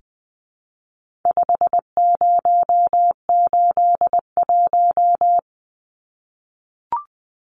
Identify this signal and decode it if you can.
Morse code, heard as a keyed tone, about 20 wpm, 700 Hz. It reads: 5081
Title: Numbers - 4-Digit 20wpm